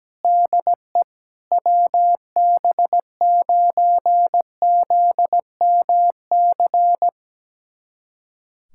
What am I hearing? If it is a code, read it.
Morse code, heard as a keyed tone, about 17 wpm, 700 Hz. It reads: DE WB9ZMC
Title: USA Callsign DE Callsign 17wpm